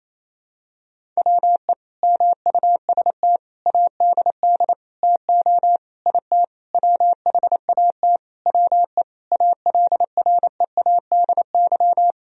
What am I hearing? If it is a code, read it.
Morse code, heard as a keyed tone, about 28 wpm, 700 Hz. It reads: WE MUST ADD TO IT WHAT WE ALREADY
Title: Sentences from Top 1000 Words - Review Part 1 - 28wpm